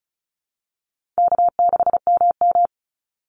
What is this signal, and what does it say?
Morse code, heard as a keyed tone, about 34 wpm, 700 Hz. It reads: K6MM
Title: Easy Call Signs Repeated 2x - 35wpm